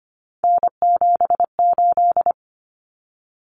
Morse code, heard as a keyed tone, about 25 wpm, 700 Hz. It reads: N78
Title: Common Cut Numbers - 3-Digit 25wpm